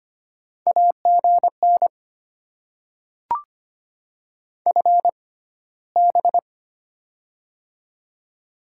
Morse code, heard as a keyed tone, about 25 wpm, spaced as for 15 wpm, 700 Hz. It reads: AGN FB
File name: Top 20 QSO Elements - 6x Character Spacing - 25wpm